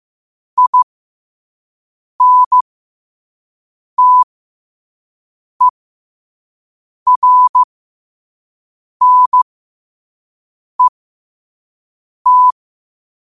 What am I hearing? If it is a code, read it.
Morse code, heard as a keyed tone, about 15 wpm, spaced as for 5 wpm, 1000 Hz. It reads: INTERNET
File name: 5wpm